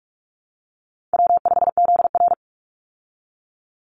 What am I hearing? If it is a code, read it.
Morse code, heard as a keyed tone, about 44 wpm, 700 Hz. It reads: W5ZR
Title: Easy Call Signs Repeated 2x - 45wpm